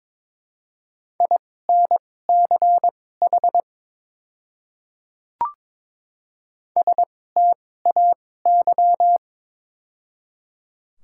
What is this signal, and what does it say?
Morse code, heard as a keyed tone, about 22 wpm, spaced as for 16 wpm, 700 Hz. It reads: INCH STAY
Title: 2x Character Spacing - Top 500 Words - 22wpm